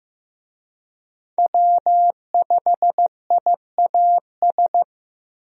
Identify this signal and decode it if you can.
Morse code, heard as a keyed tone, about 15 wpm, 700 Hz. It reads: W5IAS
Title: USA Call Signs - 5 Characters - Repeated 3x - 15wpm